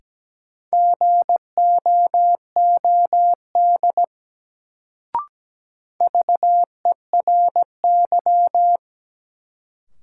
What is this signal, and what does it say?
Morse code, heard as a keyed tone, about 17 wpm, 700 Hz. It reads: GOOD VERY
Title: Sentences from Top 200 Words - New Word Drill - 17wpm